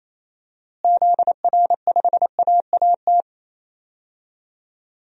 Morse code, heard as a keyed tone, about 28 wpm, 700 Hz. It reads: ZR5AAT
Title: DXCC-100 Call Signs - 6 Characters - Repeated 2x - 28wpm